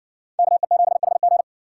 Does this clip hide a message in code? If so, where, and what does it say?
Yes: Morse code from 0.4 to 1.4 s, heard as a keyed tone, about 60 wpm, 700 Hz. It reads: KE7HG